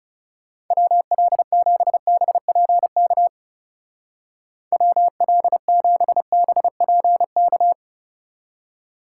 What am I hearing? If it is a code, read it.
Morse code, heard as a keyed tone, about 30 wpm, 700 Hz. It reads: WL7BPK WL7BPK
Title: Speed Racing - USA 2x3 Call Signs 30wpm